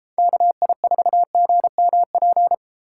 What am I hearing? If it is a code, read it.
Morse code, heard as a keyed tone, about 38 wpm, 700 Hz. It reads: KI4GMP